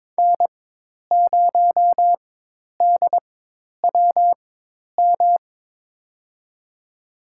Morse code, heard as a keyed tone, about 21 wpm, spaced as for 10 wpm, 700 Hz. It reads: N0DWM
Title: USA Call Signs - 5 Characters - 4x Character Spacing - 22wpm